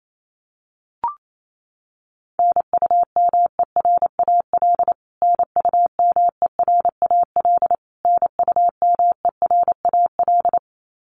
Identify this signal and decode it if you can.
Morse code, heard as a keyed tone, about 28 wpm, 700 Hz. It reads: NUMERAL NUMERAL NUMERAL
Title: Top 500 Words Repeated 28wpm